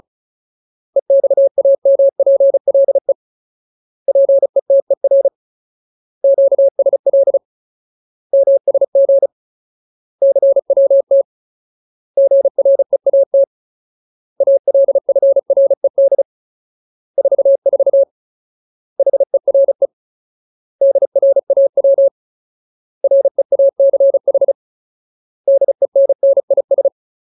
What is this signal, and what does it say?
Morse code, heard as a keyed tone, about 35 wpm, 550 Hz. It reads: EXAMPLE PETER QSL MSG CWT GREAT ALFRED 44 HERE DRAW REACH DENNIS
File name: Ultimate Word List / 35WPM Ultimate